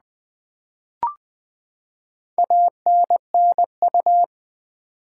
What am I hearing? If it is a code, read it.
Morse code, heard as a keyed tone, about 20 wpm, 700 Hz. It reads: ANNU
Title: Build A Word - 4 Letters At A Time - 8-Letter Words - 20wpm